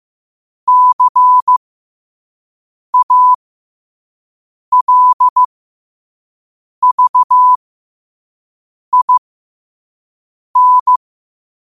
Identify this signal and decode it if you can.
Morse code, heard as a keyed tone, about 15 wpm, spaced as for 5 wpm, 1000 Hz. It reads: CALVIN
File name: qotdfiles / 5wpm